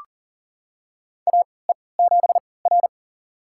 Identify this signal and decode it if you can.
Morse code, heard as a keyed tone, about 40 wpm, spaced as for 23 wpm, 700 Hz. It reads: AE7R
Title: USA Call Signs - 3x Character Spacing - 40wpm